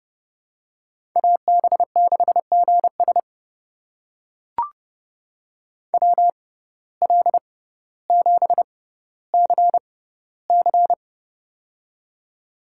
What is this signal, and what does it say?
Morse code, heard as a keyed tone, about 31 wpm, spaced as for 16 wpm, 700 Hz. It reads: AB6GS WL7CC
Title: USA Call Signs - 6x Character Spacing - 30wpm